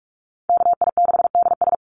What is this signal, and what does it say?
Morse code, heard as a keyed tone, about 46 wpm, 690 Hz. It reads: KI6DS
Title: NA SOTA Chaser Callsigns 45wpm